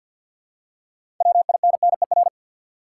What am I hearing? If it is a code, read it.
Morse code, heard as a keyed tone, about 53 wpm, 700 Hz. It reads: WINNER
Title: Six Letter Words - 50wpm